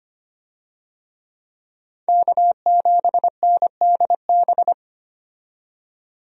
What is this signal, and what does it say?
Morse code, heard as a keyed tone, about 25 wpm, 700 Hz. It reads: K7NDB
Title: USA Call Signs - 5 Characters - 25wpm